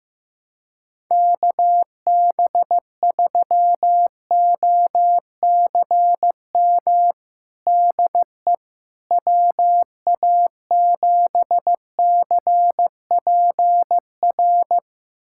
Morse code, heard as a keyed tone, about 15 wpm, 700 Hz. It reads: KB3OCM DE WA7CPR
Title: USA Callsign DE Callsign 15wpm